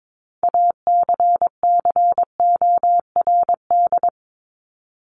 Morse code, heard as a keyed tone, about 22 wpm, 700 Hz. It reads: ACCORD
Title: ING Suffix - Top 100 Words with Base Words 22wpm